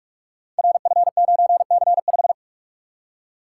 Morse code, heard as a keyed tone, about 44 wpm, 690 Hz. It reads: RF9C5
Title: Code Groups - 5 Letter:Numbers - 45wpm